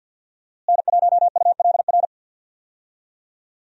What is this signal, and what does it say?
Morse code, heard as a keyed tone, about 46 wpm, 700 Hz. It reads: N1ULR
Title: USA Call Signs - 5 Characters - Repeated 2x - 50wpm